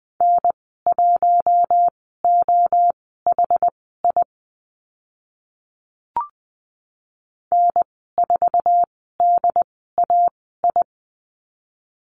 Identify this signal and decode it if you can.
Morse code, heard as a keyed tone, about 20 wpm, spaced as for 14 wpm, 700 Hz. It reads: N1OHI N4DAI
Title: 069 Lesson 12g - Letter d - Call Signs 20wpm